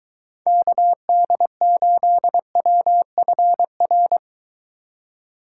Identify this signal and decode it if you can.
Morse code, heard as a keyed tone, about 23 wpm, 700 Hz. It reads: KD8WFR